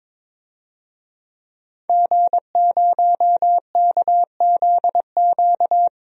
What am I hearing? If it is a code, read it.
Morse code, heard as a keyed tone, about 22 wpm, 700 Hz. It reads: G0KZQ
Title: DXCC-100 Call Signs - 5 Characters - 22wpm